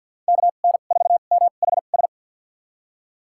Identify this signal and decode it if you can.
Morse code, heard as a keyed tone, about 47 wpm, spaced as for 35 wpm, 690 Hz. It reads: KN4MHS
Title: USA Call Signs - 2x Character Spacing - 50wpm